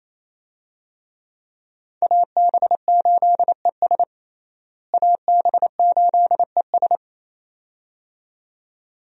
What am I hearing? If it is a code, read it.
Morse code, heard as a keyed tone, about 28 wpm, 700 Hz. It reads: AB8ES AB8ES
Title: USA Call Signs - 5 Characters - Repeated 2x - 28wpm